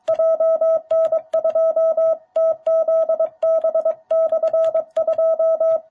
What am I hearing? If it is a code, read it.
Morse code, heard as a keyed tone, about 23 wpm, 660 Hz. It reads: JN2TZB/2
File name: jh8cbh / 2021cw